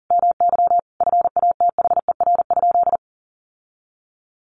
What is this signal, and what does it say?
Morse code, heard as a keyed tone, about 40 wpm, 700 Hz. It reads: MY FATHER?
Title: Sentences from Top 1400 Words 40wpm